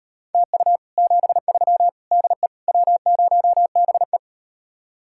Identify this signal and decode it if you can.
Morse code, heard as a keyed tone, about 38 wpm, 700 Hz. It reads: TU 73 DE W0BE